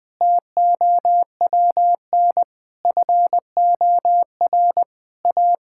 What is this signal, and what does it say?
Morse code, heard as a keyed tone, about 20 wpm, 700 Hz. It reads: TOWN FOR A
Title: Sentences from Top 1000 Words - Review Part 5 - 20wpm